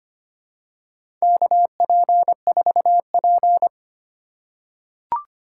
Morse code, heard as a keyed tone, about 25 wpm, 690 Hz. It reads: KP4P
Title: USA Call Signs - 4 Characters - 4x Character Spacing - 25wpm